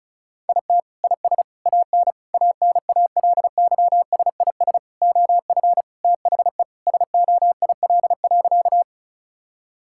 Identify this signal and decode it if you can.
Morse code, heard as a keyed tone, about 36 wpm, 700 Hz. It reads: IT IS AN ANALYSIS OF THE SOIL.
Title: Sentences from Top 1000 Words - Review Part 5 - 35wpm